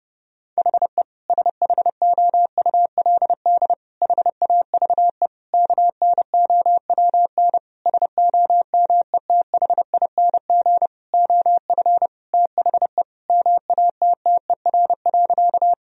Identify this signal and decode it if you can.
Morse code, heard as a keyed tone, about 30 wpm, 700 Hz. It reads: HE SHOULD HAVE KNOWN SOMETHING OF THE MATTER.
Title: Sentences from Top 1000 Words - Review Part 3 - 30wpm